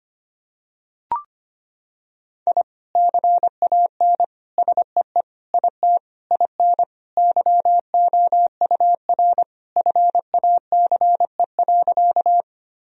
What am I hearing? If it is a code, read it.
Morse code, heard as a keyed tone, about 25 wpm, 700 Hz. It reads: I CAN SEE IT IN YOUR FACE.
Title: Sentences from Top 1000 Words - Review Part 2 - 25wpm